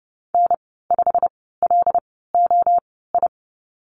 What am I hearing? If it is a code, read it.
Morse code, heard as a keyed tone, about 32 wpm, spaced as for 17 wpm, 700 Hz. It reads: N5LOI
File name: Lesson 15g - Number 5 - Call Signs 30wpm